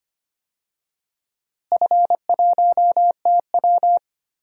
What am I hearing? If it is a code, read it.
Morse code, heard as a keyed tone, about 25 wpm, 700 Hz. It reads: F1TW
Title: DXCC-100 Call Signs - 4 Characters - Repeated 2x - 25wpm